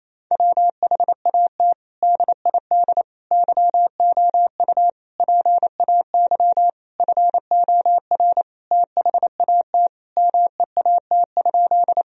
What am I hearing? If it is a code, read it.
Morse code, heard as a keyed tone, about 27 wpm, 700 Hz. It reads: WHAT DID YOU PAY FOR THAT MEAT?
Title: Sentences from Top 1000 Words - Review Part 3 - 28wpm